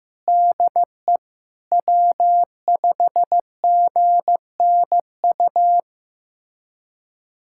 Morse code, heard as a keyed tone, about 15 wpm, 700 Hz. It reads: DE W5GNU
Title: USA Callsign DE Callsign - 5 Characters - 15wpm